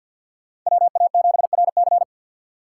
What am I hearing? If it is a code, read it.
Morse code, heard as a keyed tone, about 51 wpm, 690 Hz. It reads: WA7RC
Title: USA Call Signs - 5 Characters - 4x Character Spacing - 50wpm